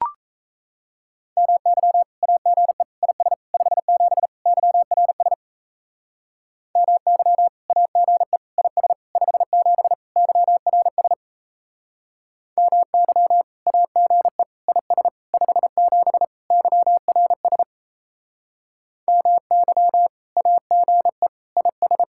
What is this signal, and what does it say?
Morse code, heard as a keyed tone, about 37 wpm, 700 Hz. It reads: MY AGE IS 57 YRS MY AGE IS 57 YRS MY AGE IS 57 YRS MY AGE IS